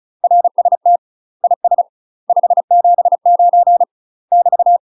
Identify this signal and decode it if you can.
Morse code, heard as a keyed tone, about 36 wpm, 700 Hz. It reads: RST IS 579 <BT>
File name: UK MASTER QSOs TRAINING / 35WPM / UK MASTER QSOs 0032